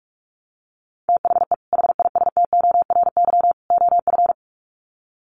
Morse code, heard as a keyed tone, about 44 wpm, 700 Hz. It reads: THE HISTORY OF
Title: Sets of 3 Words 45wpm